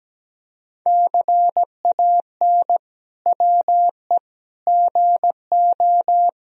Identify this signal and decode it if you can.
Morse code, heard as a keyed tone, about 17 wpm, 700 Hz. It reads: CAN WE GO
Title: Sentences from Top 1000 Words - Review Part 1 - 17wpm